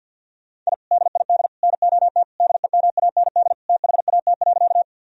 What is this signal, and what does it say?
Morse code, heard as a keyed tone, about 46 wpm, 700 Hz. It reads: I DID NOT DEMAND THAT.
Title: Sentences from Top 1000 Words - Review Part 4 - 50wpm